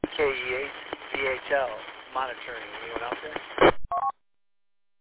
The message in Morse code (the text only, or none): none